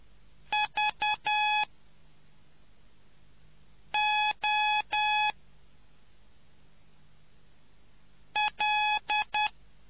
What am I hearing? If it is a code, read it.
Morse code, heard as a keyed tone, about 10 wpm, 840 Hz. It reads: V O L